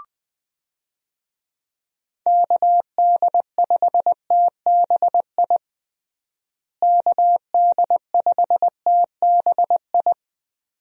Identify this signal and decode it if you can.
Morse code, heard as a keyed tone, about 20 wpm, 700 Hz. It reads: KD5TBI KD5TBI
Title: USA Call Signs - 6 Characters - Repeated 2x - 20wpm